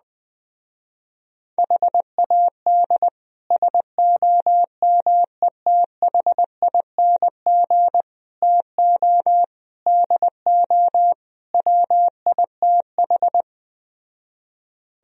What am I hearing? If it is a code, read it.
Morse code, heard as a keyed tone, about 20 wpm, 700 Hz. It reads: HAD SOMETHING TO DO WITH